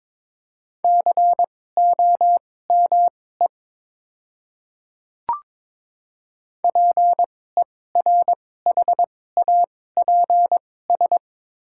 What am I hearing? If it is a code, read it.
Morse code, heard as a keyed tone, about 22 wpm, spaced as for 16 wpm, 700 Hz. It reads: COME PERHAPS
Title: 2x Character Spacing - Top 500 Words - 22wpm